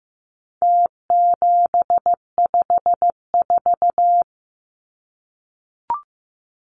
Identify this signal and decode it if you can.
Morse code, heard as a keyed tone, about 15 wpm, 700 Hz. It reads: T754
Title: Numbers - Contest Style - 4-Digit 15wpm